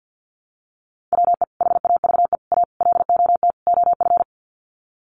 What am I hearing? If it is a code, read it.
Morse code, heard as a keyed tone, about 50 wpm, 700 Hz. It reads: WE HAVE A LOT OF
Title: Sets of 5 Words 50wpm